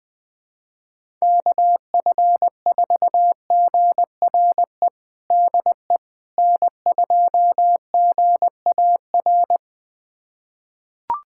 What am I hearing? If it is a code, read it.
Morse code, heard as a keyed tone, about 20 wpm, 700 Hz. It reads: KF4GRE DE N2GAR
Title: USA Callsign DE Callsign 20wpm